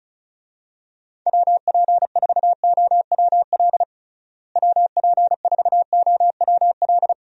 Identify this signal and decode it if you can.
Morse code, heard as a keyed tone, about 36 wpm, 700 Hz. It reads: WP4OWL WP4OWL
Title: USA Call Signs - 6 Characters - Repeated 3x - 35wpm